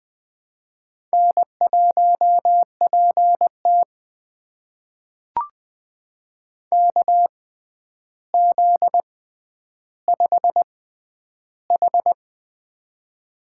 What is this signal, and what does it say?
Morse code, heard as a keyed tone, about 20 wpm, spaced as for 10 wpm, 700 Hz. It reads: N1PT KZ5H
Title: USA Call Signs - 4 Characters - 6x Character Spacing - 20wpm